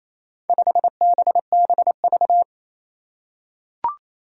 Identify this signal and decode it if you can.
Morse code, heard as a keyed tone, about 28 wpm, 700 Hz. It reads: 5BBV
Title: Four Challenging Letter-Number Combinations 28wpm